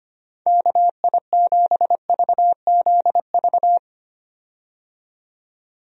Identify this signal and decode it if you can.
Morse code, heard as a keyed tone, about 25 wpm, 700 Hz. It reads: KI7VZV
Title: USA Call Signs - 6 Characters - 25wpm